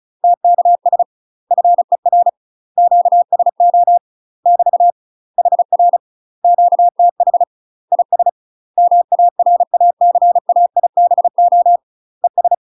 Morse code, heard as a keyed tone, about 36 wpm, 700 Hz. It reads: TKS FER QSO <BT> HR QTH IS MARACAIBO ES
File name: UK MASTER QSOs TRAINING / 35WPM / UK MASTER QSOs 0044